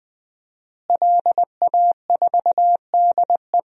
Morse code, heard as a keyed tone, about 20 wpm, 700 Hz. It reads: LA4DE
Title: DXCC-100 Call Signs - 5 Characters - Repeated 2x - 20wpm